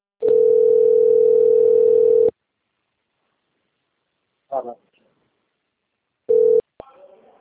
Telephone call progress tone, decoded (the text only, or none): ringback tone